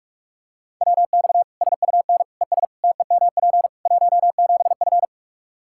Morse code, heard as a keyed tone, about 41 wpm, 700 Hz. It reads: WX SUN ES TEMP 17F
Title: Practice QSO - Exchange 2 - Info - 45wpm